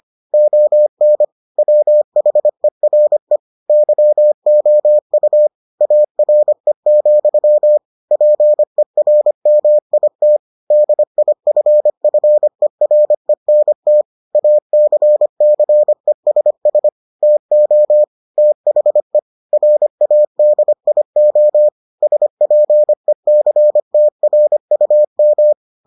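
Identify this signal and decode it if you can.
Morse code, heard as a keyed tone, about 25 wpm, 600 Hz. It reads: ON WHERE YOU ARE, PERMIT DIFFERENT ACCESS TO THE RADIO SPECTRUM
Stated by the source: Morse edition of the Foundations of Amateur Radio episodes.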